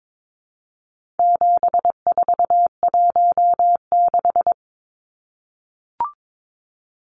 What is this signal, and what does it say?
Morse code, heard as a keyed tone, about 22 wpm, 700 Hz. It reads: 7416
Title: Numbers - 4-Digit 22wpm